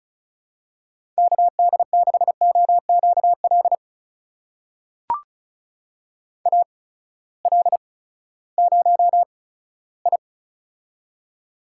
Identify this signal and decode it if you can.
Morse code, heard as a keyed tone, about 36 wpm, spaced as for 18 wpm, 700 Hz. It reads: KD6OQL AL0I
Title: USA Call Signs - 8x Character Spacing - 35wpm